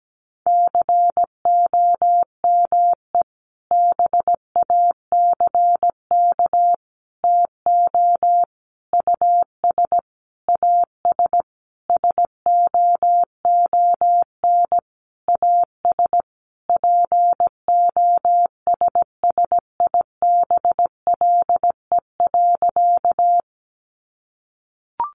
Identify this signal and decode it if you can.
Morse code, heard as a keyed tone, about 17 wpm, 700 Hz. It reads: COME BACK TO US AS SOON AS POSSIBLE.
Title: Sentences from Top 500 Words 17wpm